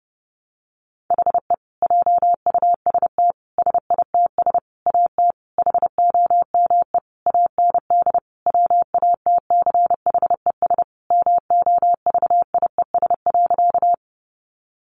Morse code, heard as a keyed tone, about 30 wpm, 700 Hz. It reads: HE JUST SITS AT HOME AND WATCHES MOVIES.
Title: Additional Easy Phrases 30wpm